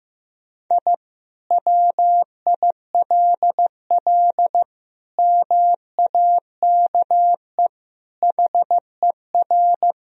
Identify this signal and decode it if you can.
Morse code, heard as a keyed tone, about 15 wpm, 700 Hz. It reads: I WILL MAKE HER
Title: Sentences from Top 1000 Words - Review Part 1 - 15wpm